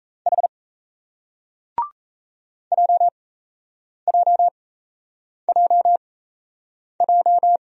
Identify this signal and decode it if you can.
Morse code, heard as a keyed tone, about 38 wpm, spaced as for 8 wpm, 700 Hz. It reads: H JJJJ